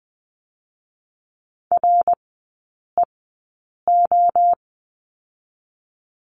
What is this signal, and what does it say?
Morse code, heard as a keyed tone, about 20 wpm, spaced as for 8 wpm, 700 Hz. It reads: REO
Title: 218 Lesson 35e - Number 0 - Three Characters 20wpm